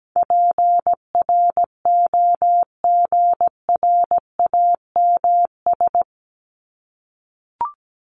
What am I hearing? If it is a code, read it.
Morse code, heard as a keyed tone, about 17 wpm, 700 Hz. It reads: PROGRAMS
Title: ES:S Suffix - Top 100 Words with Base Words 17wpm